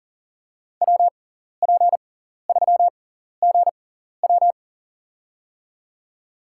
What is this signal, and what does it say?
Morse code, heard as a keyed tone, about 40 wpm, spaced as for 14 wpm, 700 Hz. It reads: WP3GW
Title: USA Call Signs - 5 Characters - 6x Character Spacing - 40wpm